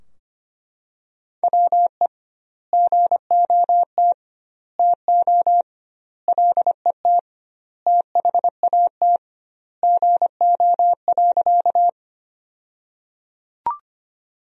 Morse code, heard as a keyed tone, about 25 wpm, 700 Hz. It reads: WE GOT TO LET THAT GO.
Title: 2x Word-Spacing - Sentences from Top 300 Words - 25wpm